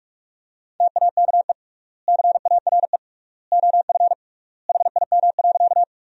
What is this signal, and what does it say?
Morse code, heard as a keyed tone, about 46 wpm, 700 Hz. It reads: TAKE CARE OF HIM.
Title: 3x Word-Spacing - Sentences from Top 300 Words - 45wpm